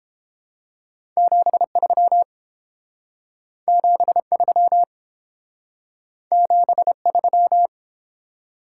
Morse code, heard as a keyed tone, about 32 wpm, 700 Hz. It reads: 73 73 73